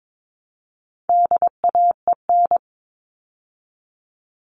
Morse code, heard as a keyed tone, about 22 wpm, 700 Hz. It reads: DAEN
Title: All Cut Numbers - 4-Digit 22wpm